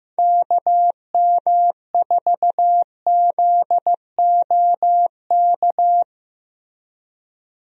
Morse code, heard as a keyed tone, about 15 wpm, 700 Hz. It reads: KM4ZOK
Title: USA Call Signs - 6 Characters - 15wpm